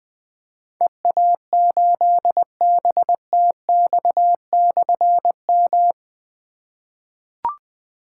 Morse code, heard as a keyed tone, about 20 wpm, 700 Hz. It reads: EA8BTX/M
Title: Difficult Call Signs 20wpm